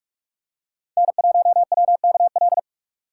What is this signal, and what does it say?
Morse code, heard as a keyed tone, about 41 wpm, 700 Hz. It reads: N1WKL